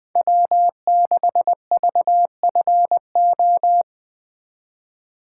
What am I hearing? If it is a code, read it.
Morse code, heard as a keyed tone, about 20 wpm, 700 Hz. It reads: W6VFO
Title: USA Call Signs - 5 Characters - 2x Character Spacing - 20wpm